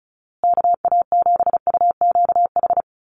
Morse code, heard as a keyed tone, about 34 wpm, 700 Hz. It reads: KA7UQH
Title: USA Call Signs Repeated 2x 35wpm